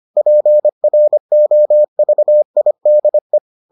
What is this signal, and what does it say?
Morse code, heard as a keyed tone, about 25 wpm, 600 Hz. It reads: PROVIDE